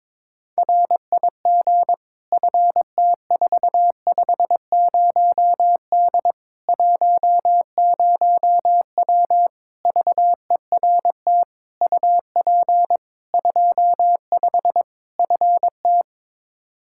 Morse code, heard as a keyed tone, about 22 wpm, 700 Hz. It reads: RIG FT450D 10W VERT UP 25 FT
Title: Practice QSO - Exchange 2 - Info - 22wpm